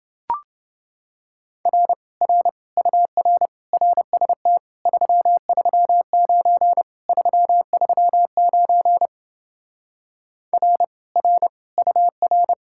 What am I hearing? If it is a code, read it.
Morse code, heard as a keyed tone, about 30 wpm, 700 Hz. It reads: R R UR RST 339 339 R R UR